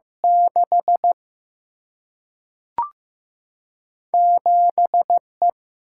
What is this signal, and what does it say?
Morse code, heard as a keyed tone, about 15 wpm, 700 Hz. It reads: 6 7 E
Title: Single-Letter Single-Number 15wpm